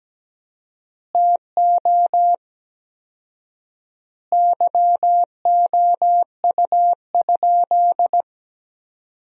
Morse code, heard as a keyed tone, about 17 wpm, 700 Hz. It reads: TO YOU?
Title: 4x Word-Spacing - Sentences from Top 100 Words - 17wpm